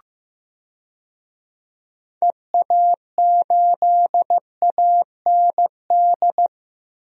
Morse code, heard as a keyed tone, about 15 wpm, 700 Hz. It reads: EA8AND
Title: DXCC-100 Call Signs - 6 Characters - Repeated 2x - 15wpm